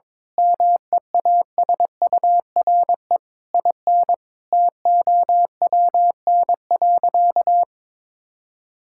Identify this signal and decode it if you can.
Morse code, heard as a keyed tone, about 22 wpm, 700 Hz. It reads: MEASURE IN TOWN.
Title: Sentences from Top 1000 Words - Review Part 5 - 22wpm